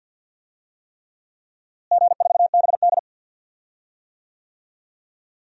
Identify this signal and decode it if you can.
Morse code, heard as a keyed tone, about 53 wpm, 700 Hz. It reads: G4BD
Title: DXCC-100 Call Signs - 4 Characters - Repeated 3x - 50wpm